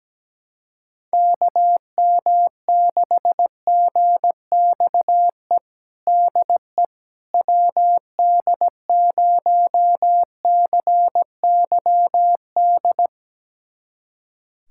Morse code, heard as a keyed tone, about 17 wpm, 700 Hz. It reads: KM6GXE DE WD0CYD
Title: USA Callsign DE Callsign - 6 Characters - 17wpm